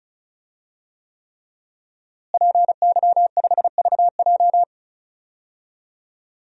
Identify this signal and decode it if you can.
Morse code, heard as a keyed tone, about 34 wpm, 700 Hz. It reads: PY5VJ
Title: DXCC-100 Call Signs - 5 Characters - Repeated 3x - 35wpm